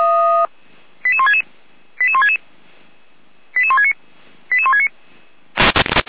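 Five-tone selective call (ZVEI), heard as quick runs of five tones, tones of about 70 ms. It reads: 88166 88166 88169 88169
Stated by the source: ZVEI five-tone selective call 88207 received by SDR, archived by Feuerwehr Orlamünde.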